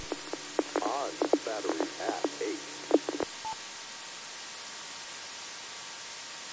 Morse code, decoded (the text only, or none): none